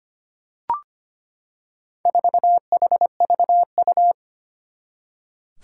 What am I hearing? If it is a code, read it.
Morse code, heard as a keyed tone, about 25 wpm, 700 Hz. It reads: 4HVU
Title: Four Challenging Letter-Number Combinations 25wpm